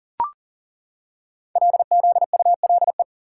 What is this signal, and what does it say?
Morse code, heard as a keyed tone, about 40 wpm, 700 Hz. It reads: LZULE
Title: Code Groups - 5 Letters - 40wpm